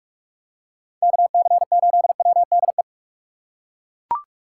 Morse code, heard as a keyed tone, about 42 wpm, 700 Hz. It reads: KC8WDE